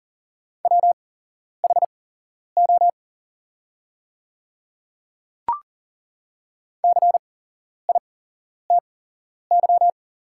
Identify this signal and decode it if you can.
Morse code, heard as a keyed tone, about 40 wpm, spaced as for 11 wpm, 690 Hz. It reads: WHO CITY
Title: 8x Character Spacing - Top 500 Words - 40wpm